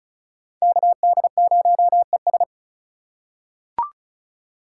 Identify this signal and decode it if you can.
Morse code, heard as a keyed tone, about 34 wpm, 700 Hz. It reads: KD0ES
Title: USA Call Signs - 5 Characters - 4x Character Spacing - 35wpm